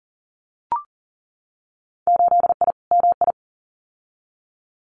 Morse code, heard as a keyed tone, about 40 wpm, 700 Hz. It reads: 8I MI
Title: Winter Field Day Exchanges Encore 40wpm